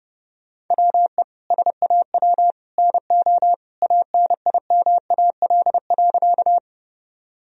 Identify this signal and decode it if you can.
Morse code, heard as a keyed tone, about 30 wpm, 700 Hz. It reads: WE SAW NO ANIMAL.
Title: Sentences from Top 1000 Words - Review Part 1 - 30wpm